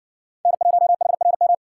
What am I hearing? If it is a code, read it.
Morse code, heard as a keyed tone, about 56 wpm, 700 Hz. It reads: N1HRM